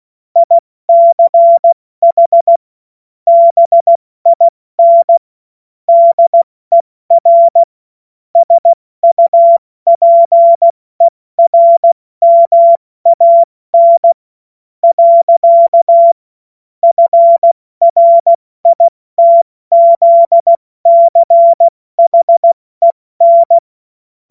Morse code, heard as a keyed tone, about 16 wpm, spaced as for 14 wpm, 680 Hz. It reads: ICH BIN DER SUPERMAN. FRITZCHEN
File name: Morsekurs / Witze / Witz 06. Juli 2021, 70 F80